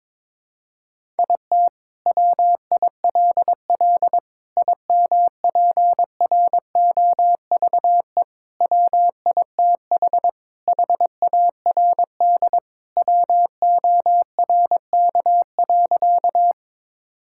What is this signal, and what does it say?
Morse code, heard as a keyed tone, about 22 wpm, 700 Hz. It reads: IT WILL IMPROVE WITH HARD WORK.
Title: Sentences from Top 1000 Words - Review Part 5 - 22wpm